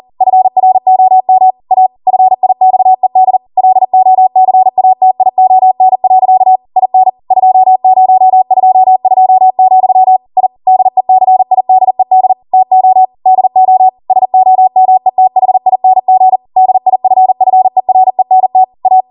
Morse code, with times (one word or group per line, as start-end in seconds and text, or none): 0.2-1.5 FROM
1.7-1.9 A
2.1-3.4 FIXED
3.6-6.6 LOCATION.
6.8-7.1 IN
7.3-10.2 2022,
10.4-10.5 I
10.7-12.3 DECIDED
12.5-13.1 TO
13.3-13.9 DO
14.1-16.4 SOMETHING
16.6-18.6 DIFFERENT
18.9-19.0 A